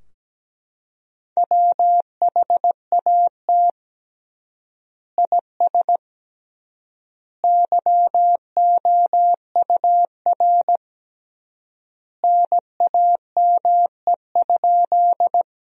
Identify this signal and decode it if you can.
Morse code, heard as a keyed tone, about 17 wpm, 700 Hz. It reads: WHAT IS YOUR NAME?
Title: 3x Word-Spacing - Sentences from Top 200 Words - 17wpm